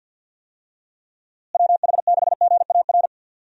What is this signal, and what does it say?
Morse code, heard as a keyed tone, about 51 wpm, 700 Hz. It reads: WH6GAR
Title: USA Call Signs - 6 Characters - Repeated 3x - 50wpm